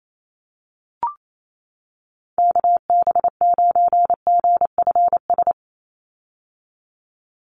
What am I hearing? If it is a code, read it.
Morse code, heard as a keyed tone, about 28 wpm, 700 Hz. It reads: KB9GFS
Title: USA Call Signs 28wpm